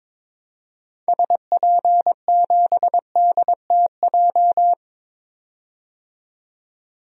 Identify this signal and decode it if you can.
Morse code, heard as a keyed tone, about 22 wpm, 700 Hz. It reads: SP7DTJ
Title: DXCC-100 Call Signs - 6 Characters - 22wpm